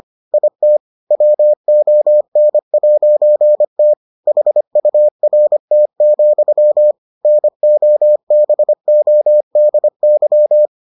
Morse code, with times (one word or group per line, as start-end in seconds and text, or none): 0.3-0.8 IT
1.1-3.9 WON'T
4.3-6.9 HURT,
7.3-10.7 NOBODY